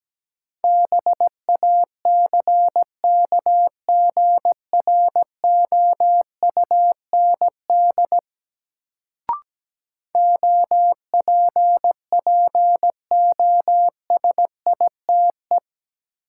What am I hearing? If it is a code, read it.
Morse code, heard as a keyed tone, about 17 wpm, 700 Hz. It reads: BACKGROUND OPPOSITE
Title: Sentences from Top 1300 Words - New Word Drill Alt - 17wpm